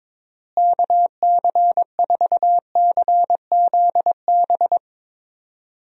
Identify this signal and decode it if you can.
Morse code, heard as a keyed tone, about 21 wpm, 700 Hz. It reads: KC4CZB
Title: USA Call Signs - 6 Characters - Repeated 2x - 22wpm